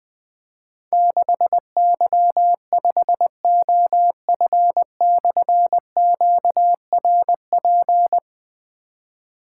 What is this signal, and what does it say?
Morse code, heard as a keyed tone, about 20 wpm, 700 Hz. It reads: 6Y5OF/QRP
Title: Difficult Call Signs 20wpm